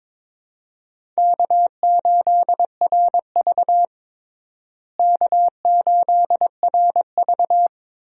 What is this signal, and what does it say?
Morse code, heard as a keyed tone, about 22 wpm, 700 Hz. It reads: K8RV K8RV
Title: USA Call Signs - 4 Characters - Repeated 2x - 22wpm